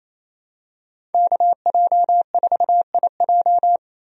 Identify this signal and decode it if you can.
Morse code, heard as a keyed tone, about 27 wpm, 700 Hz. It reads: KJ4IJ
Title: USA Call Signs - 5 Characters - 28wpm